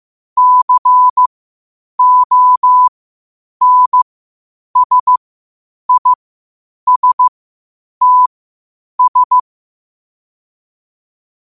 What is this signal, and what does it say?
Morse code, heard as a keyed tone, about 15 wpm, spaced as for 8 wpm, 1000 Hz. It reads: CONSISTS
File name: qotdfiles / 8wpm